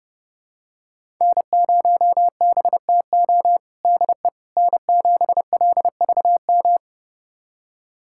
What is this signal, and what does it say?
Morse code, heard as a keyed tone, about 31 wpm, 700 Hz. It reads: N0BTO DE N7LVM
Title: USA Callsign DE Callsign - 5 Characters - 30wpm